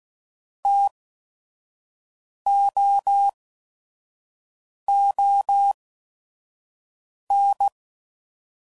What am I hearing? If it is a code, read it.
Morse code, heard as a keyed tone, about 16 wpm, spaced as for 5 wpm, 800 Hz. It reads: TOON